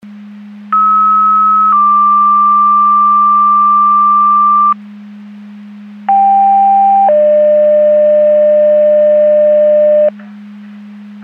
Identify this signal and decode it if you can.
Two-tone page, heard as a one-second tone, then a three-second tone, about 1290 then 1180 Hz, about 790 then 600 Hz. It is heard from 0.7 to 10.1 s.
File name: Portage / 01133 13-10 Portage 154.385 Paging